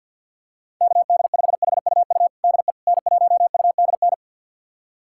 Morse code, heard as a keyed tone, about 48 wpm, 700 Hz. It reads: KD5HUU DE N1UDN
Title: USA Callsign DE Callsign 50wpm